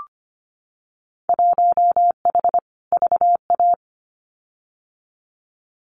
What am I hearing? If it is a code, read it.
Morse code, heard as a keyed tone, about 25 wpm, 700 Hz. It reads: 1H VA
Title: Winter Field Day Exchanges 25wpm